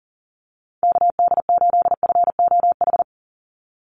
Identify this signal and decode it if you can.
Morse code, heard as a keyed tone, about 40 wpm, 690 Hz. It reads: KD8FOH
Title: USA Call Signs 40wpm